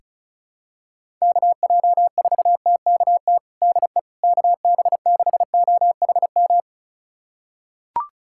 Morse code, heard as a keyed tone, about 36 wpm, 700 Hz. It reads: KJ4TKT DE KB6OHM
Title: USA Callsign DE Callsign 35wpm